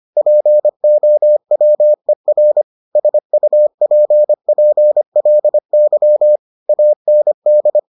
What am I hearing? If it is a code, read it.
Morse code, heard as a keyed tone, about 25 wpm, 600 Hz. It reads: POWER SUPPLY AND